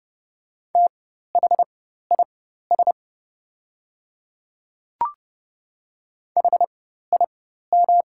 Morse code, heard as a keyed tone, about 30 wpm, spaced as for 14 wpm, 690 Hz. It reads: THIS HIM
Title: 4x Character Spacing - Top 100 Words - 30wpm